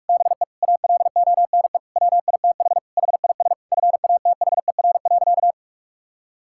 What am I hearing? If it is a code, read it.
Morse code, heard as a keyed tone, about 41 wpm, 690 Hz. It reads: BE ALONE WITH HIS FATHER.
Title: Sentences from Top 1000 Words - Review Part 5 - 45wpm